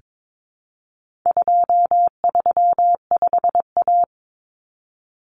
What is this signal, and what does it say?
Morse code, heard as a keyed tone, about 22 wpm, 700 Hz. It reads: 235A
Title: Common Cut Numbers - 4-Digit 22wpm